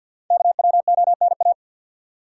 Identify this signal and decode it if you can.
Morse code, heard as a keyed tone, about 46 wpm, 700 Hz. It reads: KWONA
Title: Code Groups - 5 Letters - 50wpm